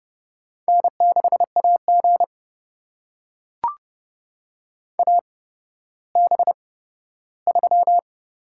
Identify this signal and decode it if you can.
Morse code, heard as a keyed tone, about 29 wpm, spaced as for 14 wpm, 700 Hz. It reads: N6AG AB3
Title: USA Call Signs - 8x Character Spacing - 30wpm